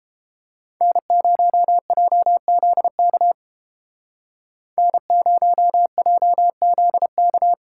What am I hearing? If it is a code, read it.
Morse code, heard as a keyed tone, about 31 wpm, 700 Hz. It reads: N0JZK N0JZK